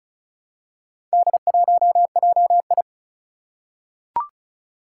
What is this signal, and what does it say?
Morse code, heard as a keyed tone, about 36 wpm, 690 Hz. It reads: D1JI
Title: Four Challenging Letter-Number Combinations 35wpm